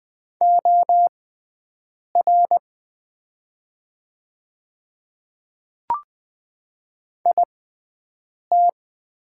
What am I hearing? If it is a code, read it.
Morse code, heard as a keyed tone, about 20 wpm, spaced as for 7 wpm, 700 Hz. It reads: OR IT
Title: 6x Character Spacing - Top 100 Words - 20wpm